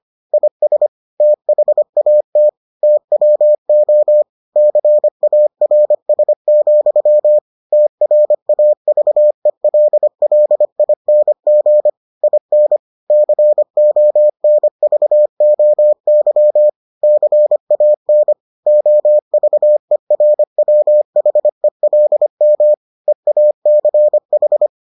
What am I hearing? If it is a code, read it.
Morse code, heard as a keyed tone, about 25 wpm, 600 Hz. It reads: IS THAT TWO CARS, TRAVELLING IN CONVOY CAN OVERWHELM EACH